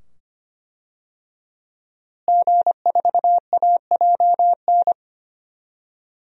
Morse code, heard as a keyed tone, about 25 wpm, 700 Hz. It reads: G4AJN
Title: DXCC-100 Call Signs - 5 Characters - Repeated 2x - 25wpm